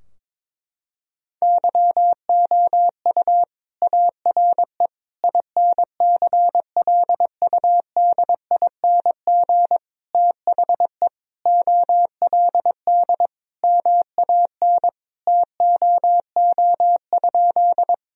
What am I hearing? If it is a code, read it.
Morse code, heard as a keyed tone, about 22 wpm, 700 Hz. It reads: YOU ARE INCLUDING THE OLD MAN TOO?
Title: Sentences from Top 1000 Words - Review Part 4 - 22wpm